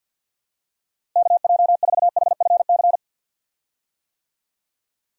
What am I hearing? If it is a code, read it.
Morse code, heard as a keyed tone, about 52 wpm, 700 Hz. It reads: KJ4HFC
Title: USA Call Signs - 6 Characters - Repeated 3x - 50wpm